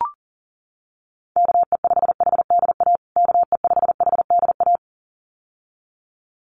Morse code, heard as a keyed tone, about 40 wpm, 700 Hz. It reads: KE5HDA KE5HDA
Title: USA Call Signs Repeated 2x 40wpm